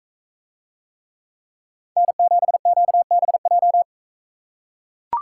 Morse code, heard as a keyed tone, about 41 wpm, 700 Hz. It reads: N7QBJ